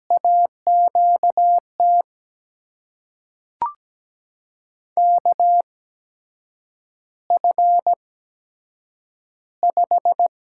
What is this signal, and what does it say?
Morse code, heard as a keyed tone, about 17 wpm, spaced as for 7 wpm, 700 Hz. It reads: AQT KF5
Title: USA Call Signs - 8x Character Spacing - 17wpm